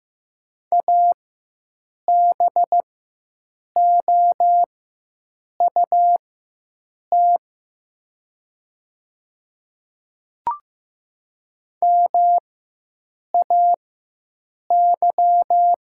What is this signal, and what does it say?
Morse code, heard as a keyed tone, about 15 wpm, spaced as for 7 wpm, 700 Hz. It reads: ABOUT MAY
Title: 4x Character Spacing - Top 100 Words - 15wpm